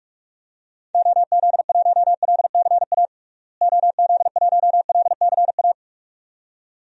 Morse code, heard as a keyed tone, about 42 wpm, 700 Hz. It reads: OZ1LCA OZ1LCA
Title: DXCC-100 Call Signs - 6 Characters - Repeated 2x - 45wpm